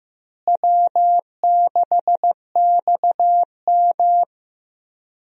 Morse code, heard as a keyed tone, about 15 wpm, 700 Hz. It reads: W6XM
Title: USA Call Signs - 4 Characters - Repeated 2x - 15wpm